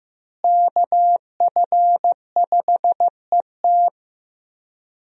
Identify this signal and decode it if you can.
Morse code, heard as a keyed tone, about 14 wpm, 700 Hz. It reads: KF5ET
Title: USA Call Signs - 5 Characters - Repeated 2x - 15wpm